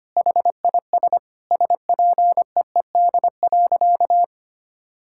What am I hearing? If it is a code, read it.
Morse code, heard as a keyed tone, about 25 wpm, 700 Hz. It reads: HIS SPEED.
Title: Sentences from Top 1000 Words - Review Part 5 - 25wpm